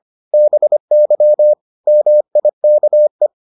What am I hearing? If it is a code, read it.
Morse code, heard as a keyed tone, about 25 wpm, 600 Hz. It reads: BY MIKE